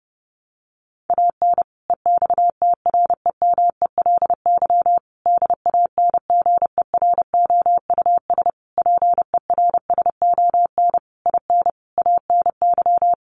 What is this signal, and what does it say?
Morse code, heard as a keyed tone, about 30 wpm, 700 Hz. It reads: AN EXTREMELY DANGEROUS PERSON IN ANY
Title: Sentences from Top 1200 Words 30wpm